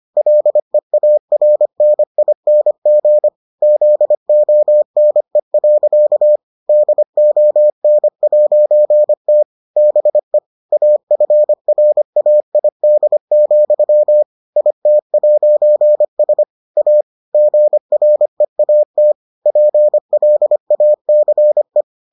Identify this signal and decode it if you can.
Morse code, heard as a keyed tone, about 25 wpm, 600 Hz. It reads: LEARNING ZONE. DON'T BE AFRAID, IT'S A GREAT PLACE